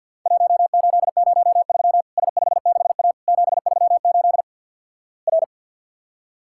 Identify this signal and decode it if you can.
Morse code, heard as a keyed tone, about 50 wpm, 700 Hz. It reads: 1803 S56A 738 R
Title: Worked All Europe DX Contest - QTC Exchange - 50wpm